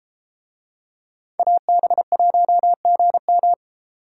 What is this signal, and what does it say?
Morse code, heard as a keyed tone, about 31 wpm, 700 Hz. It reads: AB1GM